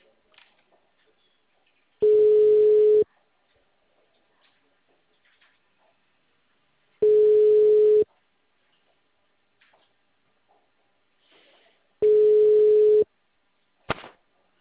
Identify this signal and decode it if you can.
Ringback tone (Europe, 425 Hz).
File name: out- -2007- - - .0